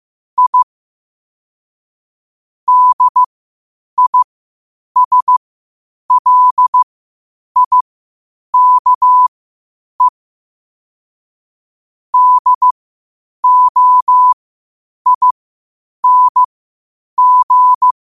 Morse code, heard as a keyed tone, about 15 wpm, spaced as for 8 wpm, 1000 Hz. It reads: I DISLIKE DOING
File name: qotdfiles / 8wpm